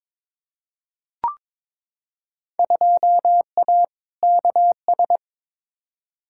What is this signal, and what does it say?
Morse code, heard as a keyed tone, about 22 wpm, 700 Hz. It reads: 2A KS
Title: Field Day Exchanges - 22wpm